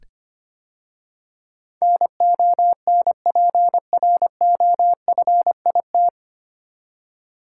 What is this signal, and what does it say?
Morse code, heard as a keyed tone, about 25 wpm, 700 Hz. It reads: NONPROFIT
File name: practice / 25wpm / 9 Letter Words - First 3 Letters Given - 25wpm